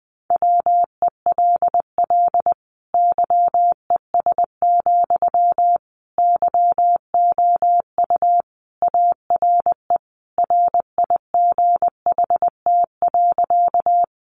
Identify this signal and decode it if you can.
Morse code, heard as a keyed tone, about 20 wpm, 700 Hz. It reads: WELL YES, YOU ARE RIGHT.
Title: Sentences from Top 500 Words 20wpm